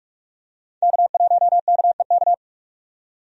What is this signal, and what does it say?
Morse code, heard as a keyed tone, about 45 wpm, 700 Hz. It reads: K1KEK
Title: USA Call Signs - 5 Characters - 45wpm